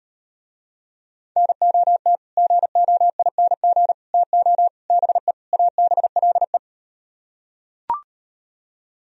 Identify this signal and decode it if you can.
Morse code, heard as a keyed tone, about 39 wpm, 700 Hz. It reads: NOT GOING TO BE ABLE